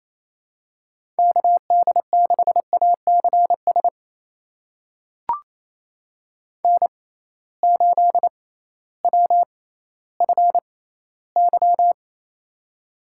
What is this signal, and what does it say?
Morse code, heard as a keyed tone, about 28 wpm, spaced as for 15 wpm, 700 Hz. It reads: KD6ACS N8WFY
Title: USA Call Signs - 6x Character Spacing - 28wpm